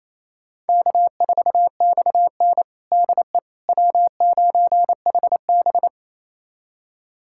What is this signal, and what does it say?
Morse code, heard as a keyed tone, about 28 wpm, 700 Hz. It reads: K4XN DE W9HB
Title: USA Callsign DE Callsign - 4 Characters - 28wpm